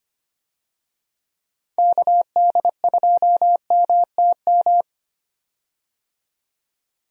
Morse code, heard as a keyed tone, about 25 wpm, 700 Hz. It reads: KD2MTM
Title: USA Call Signs - 6 Characters - 25wpm